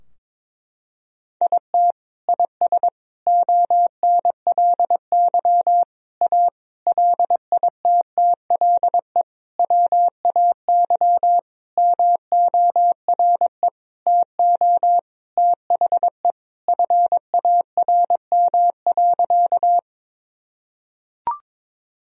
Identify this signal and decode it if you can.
Morse code, heard as a keyed tone, about 22 wpm, 700 Hz. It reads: IT IS ONLY A LITTLE WAY MORE TO THE FARM.
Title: Sentences from Top 1000 Words - Review Part 2 - 22wpm